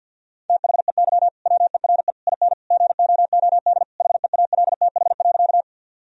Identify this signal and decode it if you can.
Morse code, heard as a keyed tone, about 49 wpm, 700 Hz. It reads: THEY WERE IN GOOD HEALTH.
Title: Sentences from Top 1000 Words - Review Part 3 - 50wpm